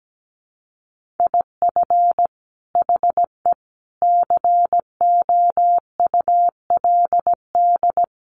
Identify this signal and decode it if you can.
Morse code, heard as a keyed tone, about 17 wpm, 700 Hz. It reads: IF HE COULD
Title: Sentences from Top 1500 Words 17wpm